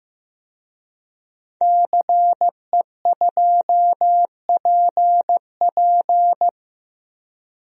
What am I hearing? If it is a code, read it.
Morse code, heard as a keyed tone, about 15 wpm, 700 Hz. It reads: CE2PP
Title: DXCC-100 Call Signs - 5 Characters - Repeated 3x - 15wpm